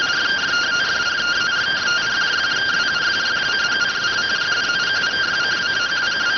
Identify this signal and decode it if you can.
Two-tone teleprinter signal, about 170 Hz shift.